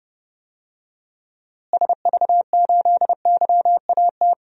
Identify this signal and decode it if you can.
Morse code, heard as a keyed tone, about 30 wpm, 700 Hz. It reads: SV8YAT
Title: DXCC-100 Call Signs - 6 Characters - Repeated 2x - 30wpm